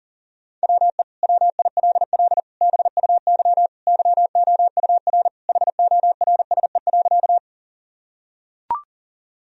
Morse code, heard as a keyed tone, about 37 wpm, 700 Hz. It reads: WE WILL BUY YOUR HORSE.
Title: Sentences from Top 1000 Words - Review Part 4 - 40wpm